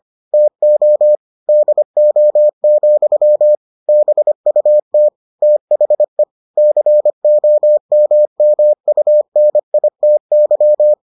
Morse code, heard as a keyed tone, about 25 wpm, 600 Hz. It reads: TO DO, BUT THE COMMUNITY